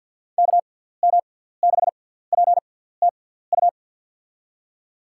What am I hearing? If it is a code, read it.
Morse code, heard as a keyed tone, about 53 wpm, spaced as for 17 wpm, 700 Hz. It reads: KM6PTU
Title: USA Call Signs - 6 Characters - 6x Character Spacing - 50wpm